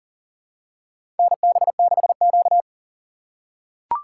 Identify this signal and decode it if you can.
Morse code, heard as a keyed tone, about 40 wpm, 700 Hz. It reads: NB6Q
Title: USA Call Signs - 4 Characters - 6x Character Spacing - 40wpm